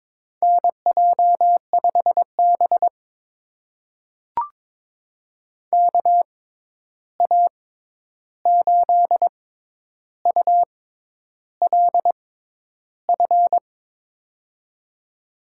Morse code, heard as a keyed tone, about 22 wpm, spaced as for 10 wpm, 700 Hz. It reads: NJ5B KA8ULF
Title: USA Call Signs - 6x Character Spacing - 22wpm